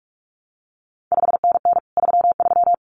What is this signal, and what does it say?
Morse code, heard as a keyed tone, about 46 wpm, 700 Hz. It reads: 5NN 33
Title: CQ WW DX Contest Exchange - 45wpm